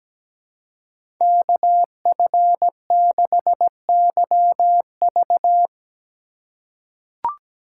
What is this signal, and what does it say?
Morse code, heard as a keyed tone, about 17 wpm, 700 Hz. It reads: KF6YV
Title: USA Call Signs - 5 Characters - 4x Character Spacing - 17wpm